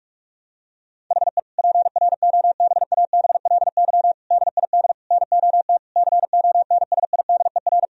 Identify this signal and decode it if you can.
Morse code, heard as a keyed tone, about 41 wpm, 700 Hz. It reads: HE PROBABLY DID NOT CONSIDER
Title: Sentences from Top 1000 Words - Review Part 4 - 45wpm